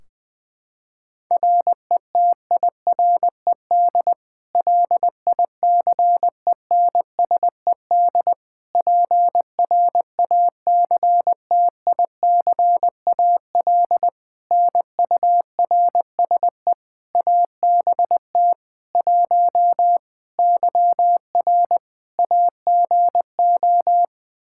Morse code, heard as a keyed tone, about 20 wpm, 700 Hz. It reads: RETIRED LICENSED PRACTICAL NURSE ABT 1 YR AGO
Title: Practice QSO - Exchange 3 - More Info - 20wpm